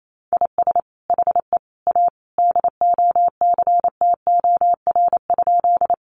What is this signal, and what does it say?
Morse code, heard as a keyed tone, about 28 wpm, 700 Hz. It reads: IS HE A DOCTOR?
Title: Sentences from Top 1500 Words 28wpm